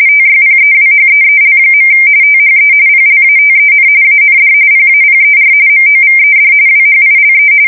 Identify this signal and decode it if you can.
RTTY, about 45.45 baud. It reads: TYPEWHEEL. THE PLATEN AND PAPER MOVED WHILE
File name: fsk170 / teletype-corp